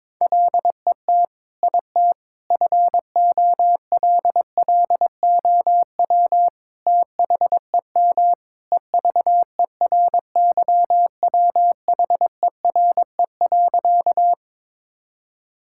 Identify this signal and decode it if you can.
Morse code, heard as a keyed tone, about 22 wpm, 700 Hz. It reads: LET IT FOLLOW THEM EVERYWHERE.
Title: Sentences from Top 1600 Words 22wpm